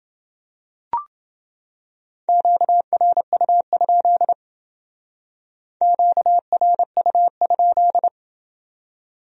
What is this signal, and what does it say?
Morse code, heard as a keyed tone, about 29 wpm, 700 Hz. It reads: QRU? QRU?